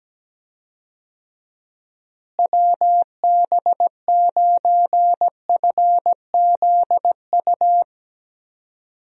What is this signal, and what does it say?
Morse code, heard as a keyed tone, about 17 wpm, 700 Hz. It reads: WB9FZU
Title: USA Call Signs - 6 Characters - Repeated 2x - 17wpm